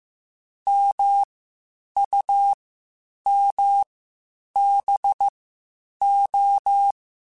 Morse code, heard as a keyed tone, about 15 wpm, spaced as for 8 wpm, 800 Hz. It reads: MUMBO
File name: Misc / Misc-mp3s / WordPairs sp @13